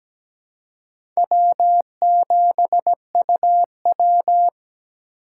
Morse code, heard as a keyed tone, about 17 wpm, 700 Hz. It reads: W7UW
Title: USA Call Signs - 4 Characters - 3x Character Spacing - 17wpm